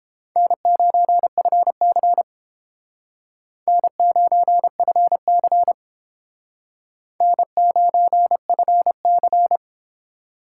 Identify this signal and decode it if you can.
Morse code, heard as a keyed tone, about 30 wpm, 700 Hz. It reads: N9FC N9FC N9FC